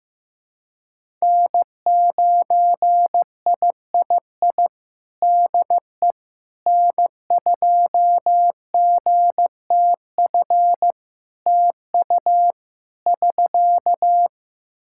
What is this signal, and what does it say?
Morse code, heard as a keyed tone, about 15 wpm, 700 Hz. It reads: N9III DE N2GTF TU <SK>
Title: Practice QSO - Exchange 4 - Ending - 15wpm